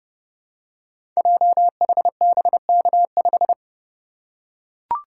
Morse code, heard as a keyed tone, about 30 wpm, 700 Hz. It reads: JHBK5
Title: Code Groups - 5 Letter:Numbers - 30wpm